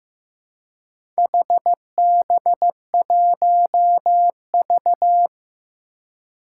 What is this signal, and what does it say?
Morse code, heard as a keyed tone, about 15 wpm, 700 Hz. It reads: HB1V
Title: Four Challenging Letter-Number Combinations 15wpm